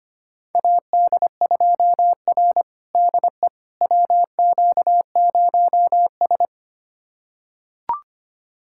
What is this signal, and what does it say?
Morse code, heard as a keyed tone, about 25 wpm, 700 Hz. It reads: AD2R DE WQ0S
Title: USA Callsign DE Callsign - 4 Characters - 25wpm